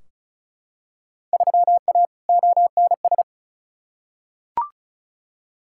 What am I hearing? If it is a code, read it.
Morse code, heard as a keyed tone, about 36 wpm, 700 Hz. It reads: 3A ONS
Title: Field Day Exchanges - 35wpm